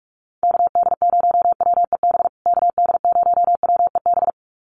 Morse code, heard as a keyed tone, about 44 wpm, 700 Hz. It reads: KD0WEB KD0WEB
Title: USA Call Signs Repeated 2x 45wpm